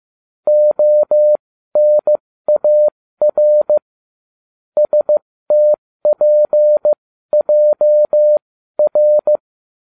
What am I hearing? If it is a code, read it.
Morse code, heard as a keyed tone, about 15 wpm, spaced as for 13 wpm, 600 Hz. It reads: ONAR STPJR